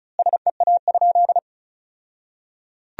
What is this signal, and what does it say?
Morse code, heard as a keyed tone, about 36 wpm, 700 Hz. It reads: SEA?
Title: 2x Word-Spacing - Sentences from Top 300 Words - 35wpm